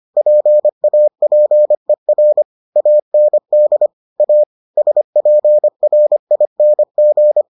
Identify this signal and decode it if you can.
Morse code, heard as a keyed tone, about 25 wpm, 600 Hz. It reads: PAPER AND A SPRING